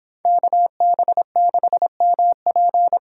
Morse code, heard as a keyed tone, about 31 wpm, 700 Hz. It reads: KB6MP